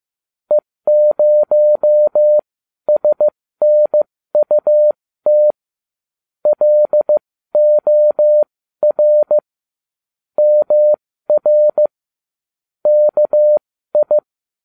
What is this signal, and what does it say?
Morse code, heard as a keyed tone, about 15 wpm, spaced as for 12 wpm, 600 Hz. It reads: E0SNUT LOR MR KI